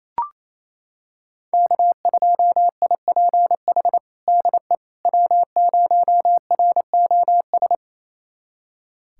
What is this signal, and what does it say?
Morse code, heard as a keyed tone, about 28 wpm, 700 Hz. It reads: K2IPH DE W0ROS
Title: USA Callsign DE Callsign - 5 Characters - 28wpm